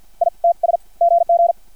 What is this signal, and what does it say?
Morse code, heard as a keyed tone, about 50 wpm, spaced as for 41 wpm, 680 Hz. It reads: ITS GG
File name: archived / CRSnet